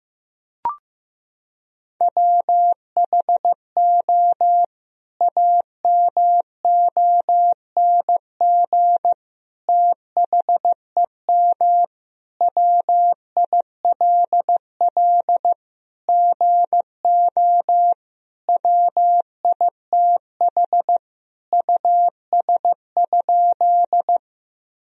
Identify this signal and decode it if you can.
Morse code, heard as a keyed tone, about 15 wpm, 700 Hz. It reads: WHO AMONG THEM WILL GO WITH US?
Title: Sentences from Top 1000 Words - Review Part 3 - 15wpm